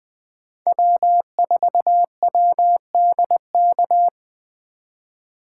Morse code, heard as a keyed tone, about 20 wpm, 700 Hz. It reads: W4WDK
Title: 2023 POTA Hunter Call Signs - 20wpm